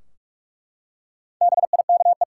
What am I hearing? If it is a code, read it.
Morse code, heard as a keyed tone, about 42 wpm, 700 Hz. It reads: BIKE
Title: Four Letter Words - 45wpm